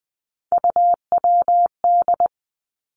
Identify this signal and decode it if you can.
Morse code, heard as a keyed tone, about 20 wpm, 700 Hz. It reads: UWD
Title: Three Challenging Letter-Number Combinations 20wpm